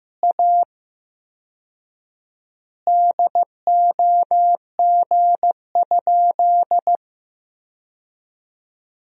Morse code, heard as a keyed tone, about 15 wpm, 700 Hz. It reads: A DOG?
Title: 4x Word-Spacing - Sentences from Top 400 Words - 15wpm